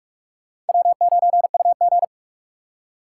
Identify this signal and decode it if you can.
Morse code, heard as a keyed tone, about 58 wpm, 700 Hz. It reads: W9UG